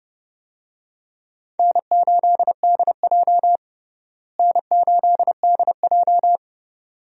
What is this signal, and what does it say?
Morse code, heard as a keyed tone, about 30 wpm, 700 Hz. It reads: N8DJ N8DJ
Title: USA Call Signs - 4 Characters - Repeated 2x - 30wpm